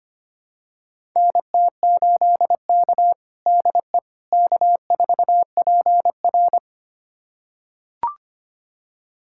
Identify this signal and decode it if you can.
Morse code, heard as a keyed tone, about 25 wpm, 700 Hz. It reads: NT8K DE K4PR
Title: USA Callsign DE Callsign - 4 Characters - 25wpm